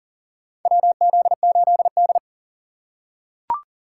Morse code, heard as a keyed tone, about 40 wpm, 700 Hz. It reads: WZ8D
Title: USA Call Signs - 4 Characters - 6x Character Spacing - 40wpm